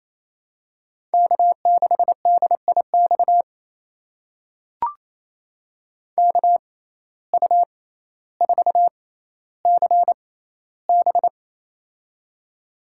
Morse code, heard as a keyed tone, about 28 wpm, spaced as for 14 wpm, 700 Hz. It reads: K6DIX KU4CB
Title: USA Call Signs - 5 Characters - 6x Character Spacing - 28wpm